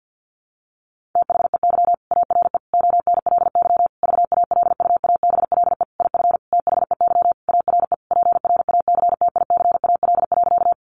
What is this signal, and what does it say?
Morse code, heard as a keyed tone, about 50 wpm, 700 Hz. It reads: THEY ARE ONLY VALUABLE IF THEY ARE PRACTICAL.
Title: Sentences from Top 1400 Words 50wpm